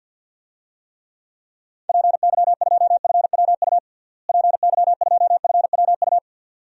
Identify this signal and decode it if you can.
Morse code, heard as a keyed tone, about 53 wpm, 700 Hz. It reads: PY2FWU PY2FWU
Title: DXCC-100 Call Signs - 6 Characters - Repeated 2x - 50wpm